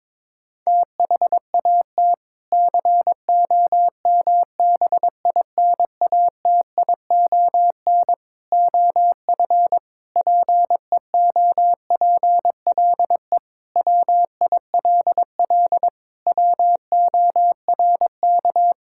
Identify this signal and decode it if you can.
Morse code, heard as a keyed tone, about 22 wpm, 700 Hz. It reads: THAT COMBINATION OF PEOPLE WILL WORK
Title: Sentences from Top 1000 Words - Review Part 6 - 22wpm